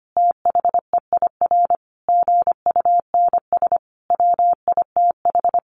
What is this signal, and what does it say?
Morse code, heard as a keyed tone, about 25 wpm, 700 Hz. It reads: THEIR GUNS WITH
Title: A Sound Of Thunder 25wpm